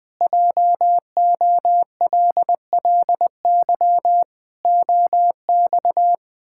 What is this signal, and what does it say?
Morse code, heard as a keyed tone, about 21 wpm, 700 Hz. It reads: JOLLY OX
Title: Pangram Challenge v2 - 20wpm